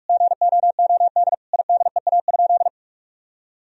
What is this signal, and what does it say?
Morse code, heard as a keyed tone, about 46 wpm, 700 Hz. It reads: GOOD IDEA?
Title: Sentences from Top 1000 Words - Review Part 4 - 45wpm